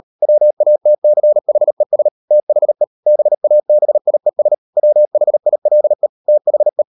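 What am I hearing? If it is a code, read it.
Morse code, heard as a keyed tone, about 38 wpm, 600 Hz. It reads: WATCHES THE BABIES WHILE THE
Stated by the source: Morse code practice at 38 WPM, sending Ken Follett's novel Never in English, from CW med Gnister.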